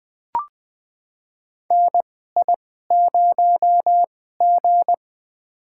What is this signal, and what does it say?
Morse code, heard as a keyed tone, about 20 wpm, spaced as for 14 wpm, 700 Hz. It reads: NI0G
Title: USA Call Signs - 4 Characters - 2x Character Spacing - 20wpm